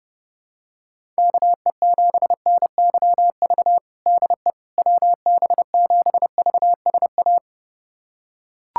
Morse code, heard as a keyed tone, about 30 wpm, 700 Hz. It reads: KE7NYV DE WB7VSA
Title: USA Callsign DE Callsign 30wpm